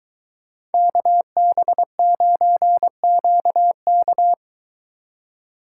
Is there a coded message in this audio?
Yes: Morse code from 0.7 to 4.3 s, heard as a keyed tone, about 22 wpm, 700 Hz. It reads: KB9QK